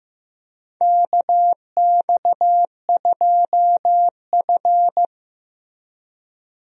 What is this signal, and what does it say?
Morse code, heard as a keyed tone, about 15 wpm, 700 Hz. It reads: KX2F
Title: USA Call Signs - 4 Characters - Repeated 2x - 15wpm